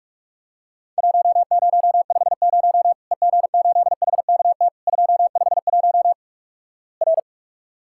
Morse code, heard as a keyed tone, about 46 wpm, 700 Hz. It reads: 1050 EG8HKT 251 R
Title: Worked All Europe DX Contest - QTC Exchange - 45wpm